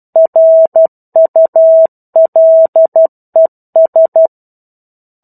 Morse code, heard as a keyed tone, about 12 wpm, 650 Hz. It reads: RULES
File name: 01-Chapter-01 12 WPM